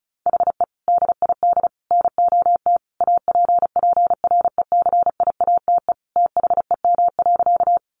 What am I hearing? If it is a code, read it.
Morse code, heard as a keyed tone, about 34 wpm, 700 Hz. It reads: HE DID NOT APPRECIATE THEM.
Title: Sentences from Top 1300 Words 35wpm